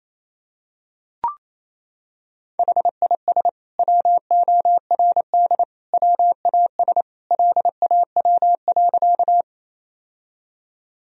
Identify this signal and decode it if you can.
Morse code, heard as a keyed tone, about 28 wpm, 700 Hz. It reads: HIS WORD WAS LAW.
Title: Sentences from Top 1000 Words - Review Part 3 - 28wpm